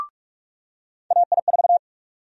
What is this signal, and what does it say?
Morse code, heard as a keyed tone, about 47 wpm, 700 Hz. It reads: AI4
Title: USA Call Sign Prefixes - 45wpm